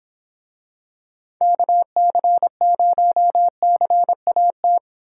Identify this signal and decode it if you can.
Morse code, heard as a keyed tone, about 24 wpm, 700 Hz. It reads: KC0CAT